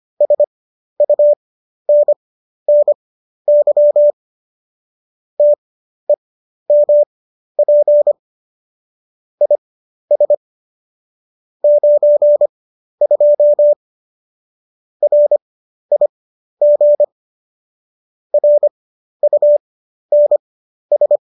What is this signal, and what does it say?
Morse code, heard as a keyed tone, about 25 wpm, spaced as for 12 wpm, 590 Hz. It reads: SUNNY TEMP IS 92 RIG RUNS